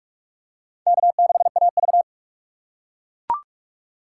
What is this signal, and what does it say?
Morse code, heard as a keyed tone, about 44 wpm, 700 Hz. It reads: K6AV
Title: USA Call Signs - 4 Characters - 6x Character Spacing - 45wpm